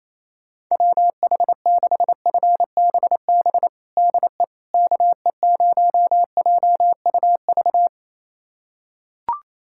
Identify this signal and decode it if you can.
Morse code, heard as a keyed tone, about 28 wpm, 700 Hz. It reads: WH6FBB DE KE0JUV
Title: USA Callsign DE Callsign 28wpm